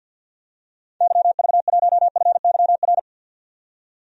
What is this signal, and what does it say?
Morse code, heard as a keyed tone, about 50 wpm, 700 Hz. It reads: YV1FYR
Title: DXCC-100 Call Signs - 6 Characters - 50wpm